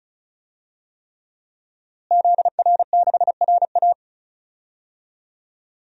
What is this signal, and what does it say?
Morse code, heard as a keyed tone, about 34 wpm, 700 Hz. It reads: ZR6RA
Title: DXCC-100 Call Signs - 5 Characters - Repeated 3x - 35wpm